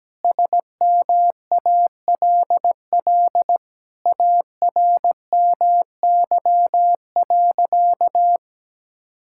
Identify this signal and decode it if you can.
Morse code, heard as a keyed tone, about 17 wpm, 700 Hz. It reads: SMALL ARMY.
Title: Sentences from Top 1000 Words - Review Part 6 - 17wpm